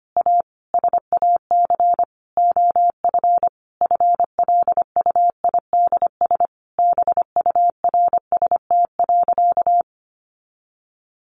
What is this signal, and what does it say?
Morse code, heard as a keyed tone, about 25 wpm, 700 Hz. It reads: A SAC OF FLUIDS BURST.
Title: A Sound Of Thunder 25wpm